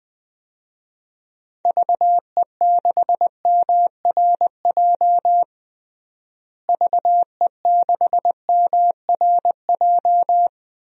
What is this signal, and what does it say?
Morse code, heard as a keyed tone, about 21 wpm, 700 Hz. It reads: VE6MRJ VE6MRJ
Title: DXCC-100 Call Signs - 6 Characters - Repeated 2x - 20wpm